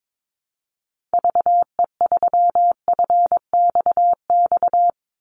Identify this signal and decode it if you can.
Morse code, heard as a keyed tone, about 22 wpm, 700 Hz. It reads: VE3FXX
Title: Call Signs Repeated 3x - 22wpm